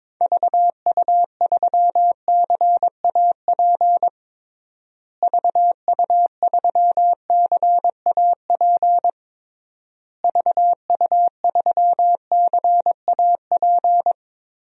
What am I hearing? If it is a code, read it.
Morse code, heard as a keyed tone, about 22 wpm, 700 Hz. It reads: VU3CAP VU3CAP VU3CAP
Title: DXCC-100 Call Signs - 6 Characters - Repeated 3x - 22wpm